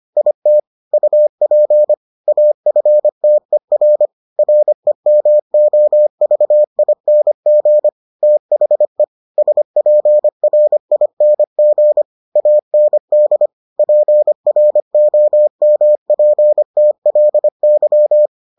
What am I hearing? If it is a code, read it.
Morse code, heard as a keyed tone, about 25 wpm, 600 Hz. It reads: IT UP AFTER REMOVING THE SPRING AND PROMPTLY